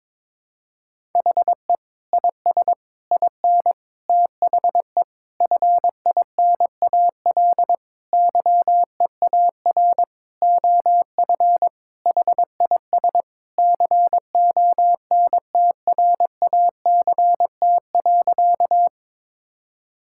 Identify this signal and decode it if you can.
Morse code, heard as a keyed tone, about 22 wpm, 700 Hz. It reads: HE IS IN THE FINAL YEAR OF HIS CONTRACT.
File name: practice / 22wpm / Sentences with 8-Word Phrases - 22wpm